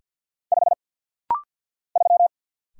Morse code, heard as a keyed tone, about 48 wpm, 700 Hz. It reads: 5 3
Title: Single Letter-Number - Rapid-Fire 50wpm